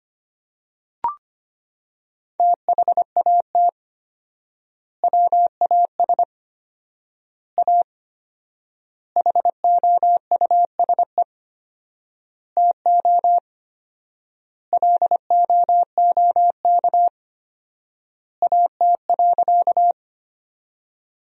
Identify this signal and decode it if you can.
Morse code, heard as a keyed tone, about 25 wpm, 700 Hz. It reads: THAT WAS A HOUSE TO LOOK AT.
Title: 4x Word-Spacing - Sentences from Top 200 Words - 25wpm